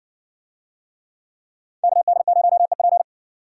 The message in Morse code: KD0EP

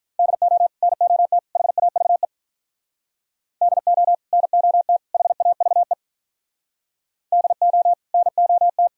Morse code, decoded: DO NOT HAVE DO NOT HAVE DO NOT

